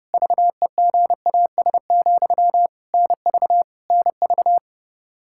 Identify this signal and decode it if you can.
Morse code, heard as a keyed tone, about 30 wpm, 700 Hz. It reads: VEGAS, NV NV
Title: K1USN SST QSO Practice 30wpm